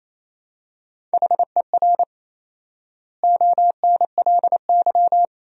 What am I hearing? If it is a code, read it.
Morse code, heard as a keyed tone, about 28 wpm, 700 Hz. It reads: HER ONLY
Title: 4x Word-Spacing - Sentences from Top 400 Words - 28wpm